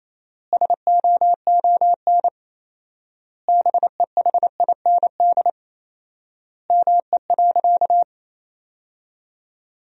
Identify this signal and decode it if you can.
Morse code, heard as a keyed tone, about 28 wpm, 700 Hz. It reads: SOON BEHIND ME.
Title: 4x Word-Spacing - Sentences from Top 500 Words - 28wpm